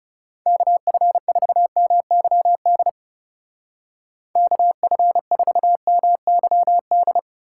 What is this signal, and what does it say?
Morse code, heard as a keyed tone, about 35 wpm, 700 Hz. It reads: KF4MYD KF4MYD
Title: Speed Racing - USA 2x3 Call Signs 30wpm